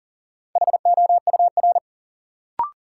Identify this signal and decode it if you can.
Morse code, heard as a keyed tone, about 38 wpm, 700 Hz. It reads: HOUR
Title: Sets of 5 Words - New Word Drill - 40wpm